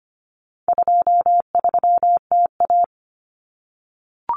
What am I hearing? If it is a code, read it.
Morse code, heard as a keyed tone, about 25 wpm, 700 Hz. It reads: 23TA
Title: Common Cut Numbers - 4-Digit 25wpm